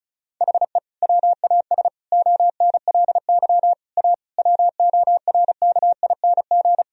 Morse code, heard as a keyed tone, about 34 wpm, 700 Hz. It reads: HE WAS ONLY A WORKING
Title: Sentences from Top 1000 Words - Review Part 4 - 35wpm